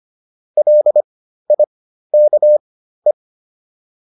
Morse code, heard as a keyed tone, about 25 wpm, spaced as for 13 wpm, 600 Hz. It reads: LIKE